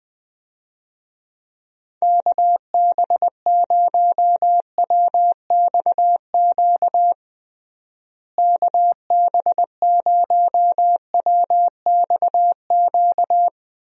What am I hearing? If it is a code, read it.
Morse code, heard as a keyed tone, about 21 wpm, 700 Hz. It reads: KB0WXQ KB0WXQ
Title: USA Call Signs - 6 Characters - Repeated 2x - 20wpm